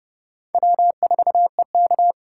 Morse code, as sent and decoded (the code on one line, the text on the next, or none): .-- ....- . -.-
W4EK